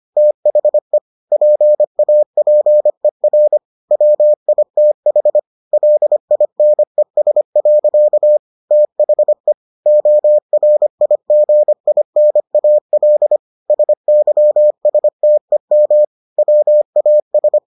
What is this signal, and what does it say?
Morse code, heard as a keyed tone, about 25 wpm, 600 Hz. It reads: THE PAPER WITH LINES. THE ORIGINAL SYSTEM WAS